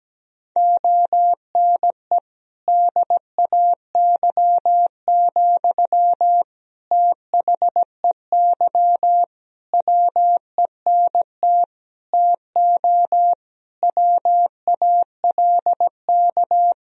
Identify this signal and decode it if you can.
Morse code, heard as a keyed tone, about 17 wpm, 700 Hz. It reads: ONE DAY, THEY WENT TO WALK
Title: Sentences from Top 1000 Words - Review Part 2 - 17wpm